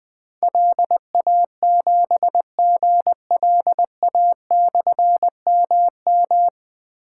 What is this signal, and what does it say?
Morse code, heard as a keyed tone, about 20 wpm, 700 Hz. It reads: LA7GLA/MM
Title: Difficult Call Signs 20wpm